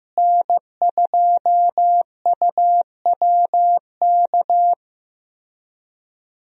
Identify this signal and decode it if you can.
Morse code, heard as a keyed tone, about 16 wpm, 700 Hz. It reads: N2UWK
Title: USA Call Signs - 5 Characters - Repeated 3x - 15wpm